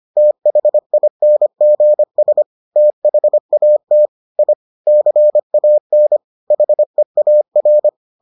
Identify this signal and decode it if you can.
Morse code, heard as a keyed tone, about 25 wpm, 600 Hz. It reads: THINGS THAT I CAN HEAR